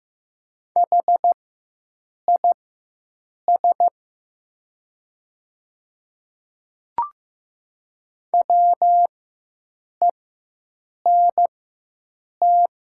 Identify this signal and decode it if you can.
Morse code, heard as a keyed tone, about 15 wpm, spaced as for 7 wpm, 700 Hz. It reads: HIS WENT
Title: 4x Character Spacing - Top 200 Words - 15wpm